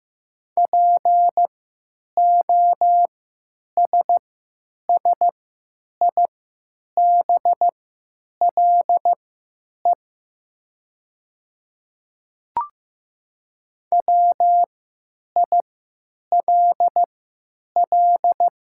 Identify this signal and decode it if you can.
Morse code, heard as a keyed tone, about 15 wpm, spaced as for 9 wpm, 700 Hz. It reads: POSSIBLE WILL
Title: 3x Character Spacing - Top 500 Words - 15wpm